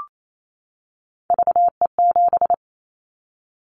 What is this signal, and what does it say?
Morse code, heard as a keyed tone, about 28 wpm, 700 Hz. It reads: VE7
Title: Callsign Prefixes for Top 100 DXCC - 28wpm